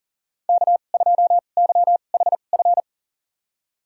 Morse code, heard as a keyed tone, about 37 wpm, spaced as for 28 wpm, 700 Hz. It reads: K2YHF
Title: USA Call Signs - 5 Characters - 2x Character Spacing - 40wpm